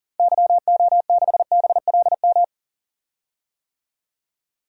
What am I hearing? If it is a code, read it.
Morse code, heard as a keyed tone, about 40 wpm, 700 Hz. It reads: YO6BLM
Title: DXCC-100 Call Signs - 6 Characters - 40wpm